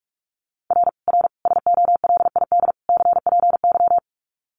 Morse code, heard as a keyed tone, about 44 wpm, 700 Hz. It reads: R R SOLID CPY
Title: Common QSO Phrases 45wpm